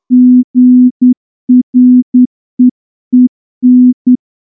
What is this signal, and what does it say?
Morse code, heard as a keyed tone, about 9 wpm, 250 Hz. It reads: GREEN
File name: morse-training-10WPM